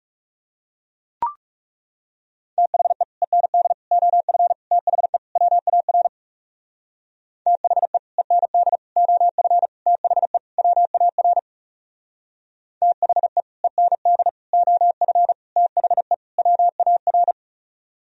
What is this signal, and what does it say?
Morse code, heard as a keyed tone, about 37 wpm, 700 Hz. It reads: THE END OF THE WAR THE END OF THE WAR THE END OF THE WAR
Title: Speed-Racing - Sets of 5 Words 30wpm